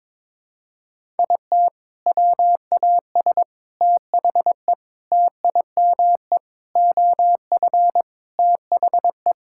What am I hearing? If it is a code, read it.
Morse code, heard as a keyed tone, about 22 wpm, 700 Hz. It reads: IT WAS THE TIME OF THE
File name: practice / 22wpm / Sentences with 4-Word Phrases - 22wpm